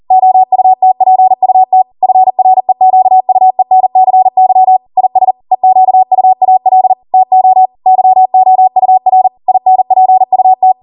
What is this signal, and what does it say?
Morse code, heard as a keyed tone, about 40 wpm, 750 Hz. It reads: OUTPUT FREQUENCY IS EQUAL TO YOUR INPUT
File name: Morse / 40WPM